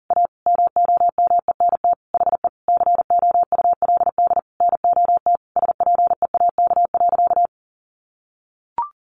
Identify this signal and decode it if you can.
Morse code, heard as a keyed tone, about 37 wpm, 700 Hz. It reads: A MOMENT HE COULD NOT SPEAK.
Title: Sentences from Top 1100 Words 40wpm